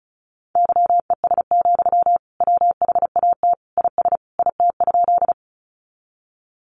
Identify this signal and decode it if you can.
Morse code, heard as a keyed tone, about 34 wpm, 700 Hz. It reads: YES, WHAT IS IT?
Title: Sentences from Top 500 Words 35wpm